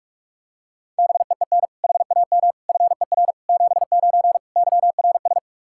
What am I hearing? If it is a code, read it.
Morse code, heard as a keyed tone, about 46 wpm, 700 Hz. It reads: BEEN HAM FER 79 YRS
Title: Practice QSO - Exchange 3 - More Info - 45wpm